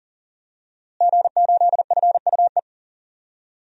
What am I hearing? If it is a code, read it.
Morse code, heard as a keyed tone, about 38 wpm, 700 Hz. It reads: G8FUE
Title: DXCC-100 Call Signs - 5 Characters - Repeated 3x - 40wpm